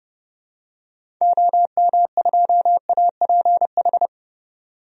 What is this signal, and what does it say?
Morse code, heard as a keyed tone, about 30 wpm, 700 Hz. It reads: OM2APH
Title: DXCC-100 Call Signs - 6 Characters - 30wpm